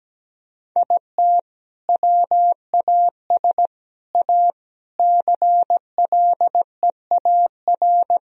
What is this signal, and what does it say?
Morse code, heard as a keyed tone, about 17 wpm, 700 Hz. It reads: IT WAS A CLEAR
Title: Sentences from Top 1000 Words - Review Part 2 - 17wpm